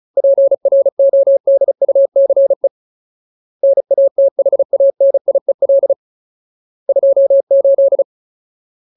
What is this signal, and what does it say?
Morse code, heard as a keyed tone, about 35 wpm, 550 Hz. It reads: PRODUCE NATHANIEL 28
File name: Ultimate Word List / 35WPM Ultimate